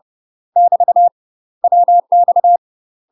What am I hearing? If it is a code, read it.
Morse code, heard as a keyed tone, about 30 wpm, 700 Hz. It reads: <BT> WX
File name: UK MASTER QSOs TRAINING / 30WPM / UK MASTER QSOs 0004